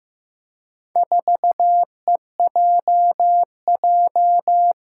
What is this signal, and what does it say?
Morse code, heard as a keyed tone, about 15 wpm, 700 Hz. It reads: 4EJJ
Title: Four Challenging Letter-Number Combinations 15wpm